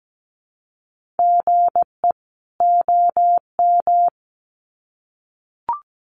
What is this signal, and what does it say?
Morse code, heard as a keyed tone, about 17 wpm, 700 Hz. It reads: GE OM
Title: Common QSO Phrases 17wpm